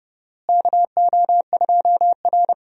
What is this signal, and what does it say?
Morse code, heard as a keyed tone, about 30 wpm, 700 Hz. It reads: KO2R
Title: USA Call Signs - 4 Characters - Repeated 3x - 30wpm